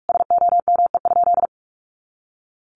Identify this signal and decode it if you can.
Morse code, heard as a keyed tone, about 44 wpm, 700 Hz. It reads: SOME?
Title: Sentences from Top 100 Words 45wpm